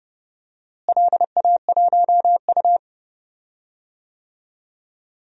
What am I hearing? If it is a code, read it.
Morse code, heard as a keyed tone, about 32 wpm, 700 Hz. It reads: LA1U
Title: DXCC-100 Call Signs - 4 Characters - Repeated 3x - 30wpm